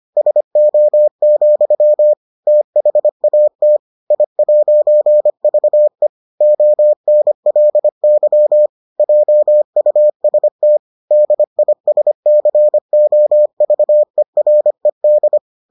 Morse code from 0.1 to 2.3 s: SO,

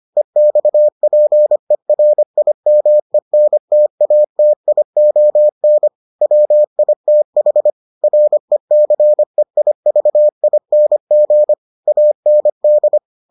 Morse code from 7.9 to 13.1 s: RECEIVING AND

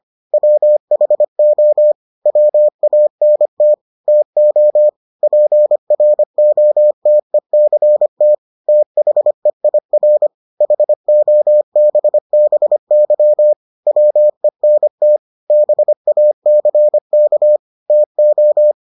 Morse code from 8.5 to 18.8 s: THEIR HOBBY WENT BACK TO

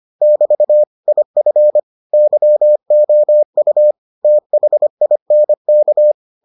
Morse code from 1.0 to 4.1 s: IF YOU

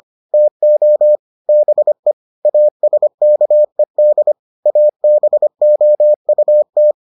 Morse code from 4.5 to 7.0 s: ABOUT

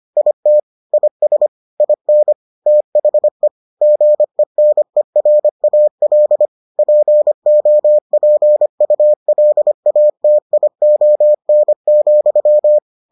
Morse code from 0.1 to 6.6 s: IT IS IN THE GENERAL

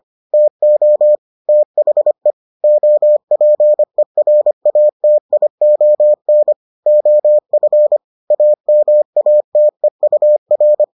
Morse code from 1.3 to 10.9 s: THE OPERATION OF AMATEUR

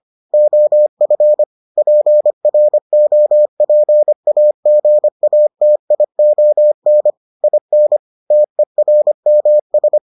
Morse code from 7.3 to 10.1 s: IN TERMS

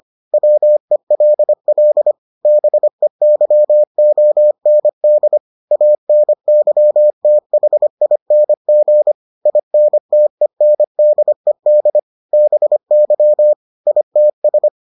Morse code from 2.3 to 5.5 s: BEYOND

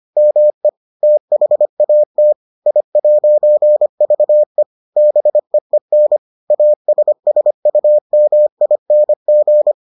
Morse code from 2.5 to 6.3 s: I'VE BEEN

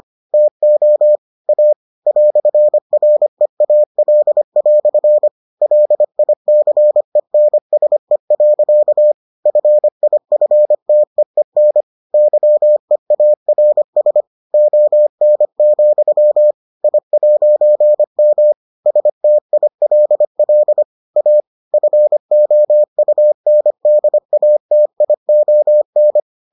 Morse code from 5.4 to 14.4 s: LICENSE. FIFTEEN YEARS

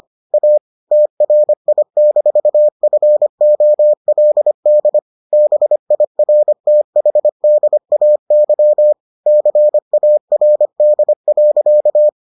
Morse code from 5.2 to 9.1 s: BIRTHDAY

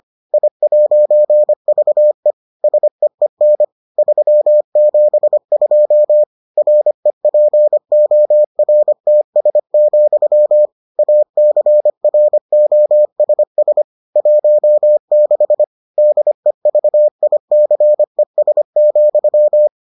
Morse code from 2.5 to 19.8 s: SEEN 372 REPORTS, ACROSS 16 DEVICES,

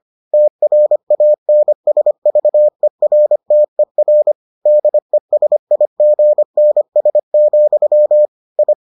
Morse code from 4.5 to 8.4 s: DESIGNS,

